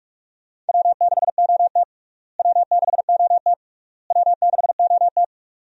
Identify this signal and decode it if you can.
Morse code, heard as a keyed tone, about 45 wpm, 700 Hz. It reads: W6OT W6OT W6OT
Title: USA Call Signs - 4 Characters - Repeated 3x - 45wpm